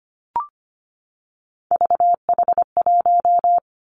Morse code, heard as a keyed tone, about 25 wpm, 700 Hz. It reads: VH1
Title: Three Challenging Letter-Number Combinations 25wpm